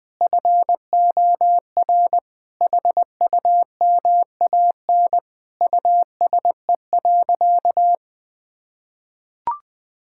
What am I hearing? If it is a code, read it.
Morse code, heard as a keyed tone, about 20 wpm, 700 Hz. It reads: FOR HUMAN USE.
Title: Sentences from Top 1000 Words - Review Part 6 - 20wpm